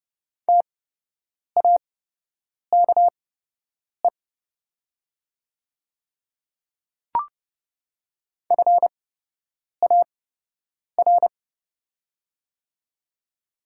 Morse code, heard as a keyed tone, about 31 wpm, spaced as for 8 wpm, 700 Hz. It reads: TAKE FAR
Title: 8x Character Spacing - Top 300 Words - 30wpm